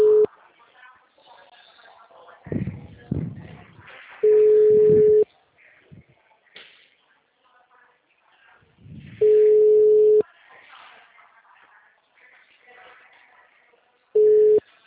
Ringback tone (Europe, 425 Hz).
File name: out- -2025- - - .57612